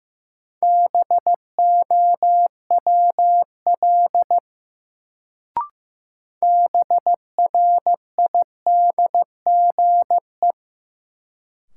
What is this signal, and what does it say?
Morse code, heard as a keyed tone, about 15 wpm, 700 Hz. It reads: BOWL BRIDGE
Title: Sentences from Top 1200 Words - New Word Drill Alt - 15wpm